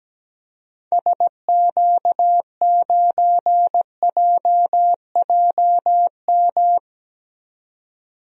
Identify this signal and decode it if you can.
Morse code, heard as a keyed tone, about 17 wpm, 700 Hz. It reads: SQ9JJM
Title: DXCC-100 Call Signs - 6 Characters - 17wpm